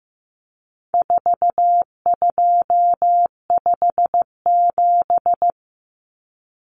Morse code, heard as a keyed tone, about 15 wpm, 700 Hz. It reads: 4257
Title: Numbers - 4-Digit 15wpm